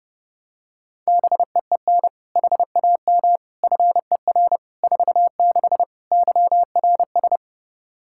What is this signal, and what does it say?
Morse code, heard as a keyed tone, about 30 wpm, 700 Hz. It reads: BEEN HAM FER 46 YRS
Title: K1USN SST QSO Practice 30wpm